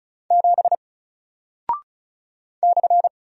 Morse code, heard as a keyed tone, about 36 wpm, 700 Hz. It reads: 7 /
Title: Single Character (Letter:Number:Punctuation:Prosign) Alt 35wpm